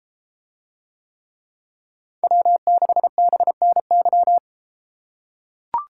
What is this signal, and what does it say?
Morse code, heard as a keyed tone, about 30 wpm, 700 Hz. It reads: W6BNY